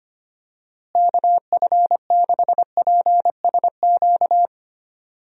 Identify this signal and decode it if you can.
Morse code, heard as a keyed tone, about 25 wpm, 700 Hz. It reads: KF6PSQ
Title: USA Call Signs - 6 Characters - Repeated 2x - 25wpm